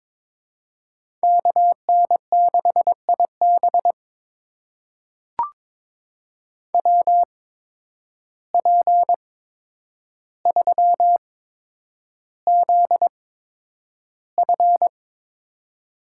Morse code, heard as a keyed tone, about 22 wpm, spaced as for 9 wpm, 700 Hz. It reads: KN6IB WP3ZF
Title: USA Call Signs - 5 Characters - 8x Character Spacing - 22wpm